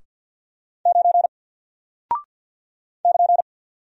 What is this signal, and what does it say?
Morse code, heard as a keyed tone, about 49 wpm, 700 Hz. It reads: 9 <KN>
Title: Single Character (Letter:Number:Punctuation:Prosign) Alt 50wpm